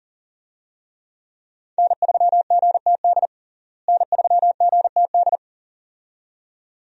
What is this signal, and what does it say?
Morse code, heard as a keyed tone, about 40 wpm, 700 Hz. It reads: N3GTD N3GTD
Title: USA Call Signs - 5 Characters - Repeated 2x - 40wpm